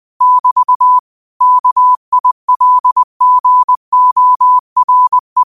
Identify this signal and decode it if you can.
Morse code, heard as a keyed tone, about 20 wpm, 1000 Hz. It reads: <BT> KILGORE